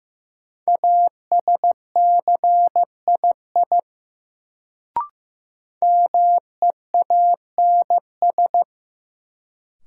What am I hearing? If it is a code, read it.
Morse code, heard as a keyed tone, about 15 wpm, 700 Hz. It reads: ASCII MEANS
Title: Sets of 5 Words - New Word Drill - 15wpm